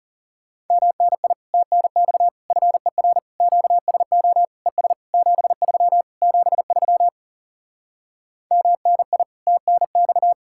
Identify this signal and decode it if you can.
Morse code, heard as a keyed tone, about 40 wpm, 700 Hz. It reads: MNI TNX FER QSO ES 73 73 MNI TNX